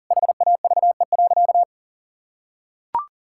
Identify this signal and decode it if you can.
Morse code, heard as a keyed tone, about 40 wpm, 700 Hz. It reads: HAVE.
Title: Sentences from Top 1000 Words - Review Part 5 - 40wpm